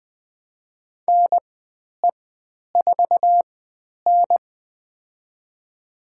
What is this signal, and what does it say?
Morse code, heard as a keyed tone, about 20 wpm, spaced as for 10 wpm, 700 Hz. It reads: NE4N
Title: 095 Lesson 16g - Letter c - Call Signs 20wpm